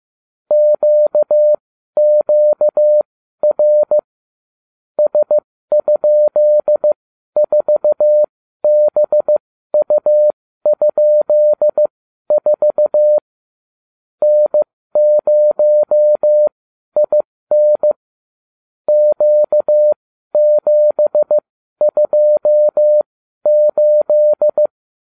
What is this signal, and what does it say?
Morse code, heard as a keyed tone, about 15 wpm, spaced as for 12 wpm, 600 Hz. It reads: QQR S?4BU?4 N0IN Q728